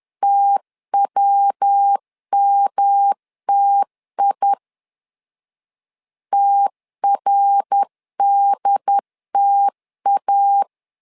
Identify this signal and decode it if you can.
Morse code, heard as a keyed tone, about 10 wpm, 800 Hz. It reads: TWMTI TRDTA
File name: LEZIONI / 3 MWRO / 3 MWR 10wpm 1952